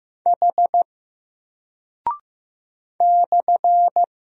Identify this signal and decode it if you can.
Morse code, heard as a keyed tone, about 15 wpm, 700 Hz. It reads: H /
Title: Single Character (Letter:Number:Punctuation:Prosign) Alt 15wpm